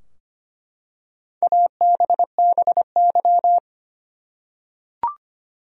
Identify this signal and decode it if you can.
Morse code, heard as a keyed tone, about 25 wpm, 700 Hz. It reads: ABBY
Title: USA Common Names - Max 4 Letters - 25wpm